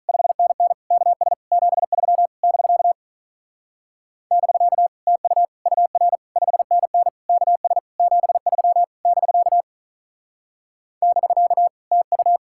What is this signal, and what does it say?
Morse code, heard as a keyed tone, about 43 wpm, 700 Hz. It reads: 5NN KS 73 <BK> <BK> TU UR 5NN KS 73 <BK> <BK> TU
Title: Speed-Racing Common QSO Phrases - 35wpm